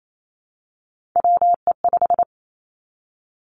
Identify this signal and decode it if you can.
Morse code, heard as a keyed tone, about 28 wpm, 700 Hz. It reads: WE5
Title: Three Challenging Letter-Number Combinations 28wpm